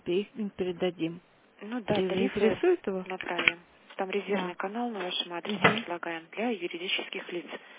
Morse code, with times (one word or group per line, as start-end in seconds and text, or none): none